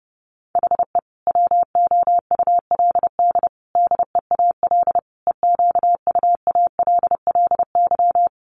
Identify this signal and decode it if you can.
Morse code, heard as a keyed tone, about 30 wpm, 700 Hz. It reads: HE WOULD DEAL EQUALLY
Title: Sentences from Top 1400 Words 30wpm